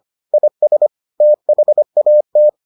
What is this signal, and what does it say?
Morse code, heard as a keyed tone, about 25 wpm, 600 Hz. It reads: IS THAT